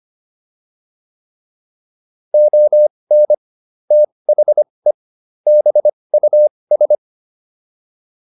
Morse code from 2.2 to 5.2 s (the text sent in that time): ON THE